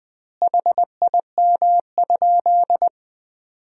Morse code, heard as a keyed tone, about 20 wpm, 700 Hz. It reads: HIM?
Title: 2x Word-Spacing - Sentences from Top 100 Words - 20wpm